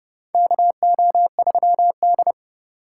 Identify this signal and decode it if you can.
Morse code, heard as a keyed tone, about 30 wpm, 700 Hz. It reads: KO3D
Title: USA Call Signs - 4 Characters - 2x Character Spacing - 30wpm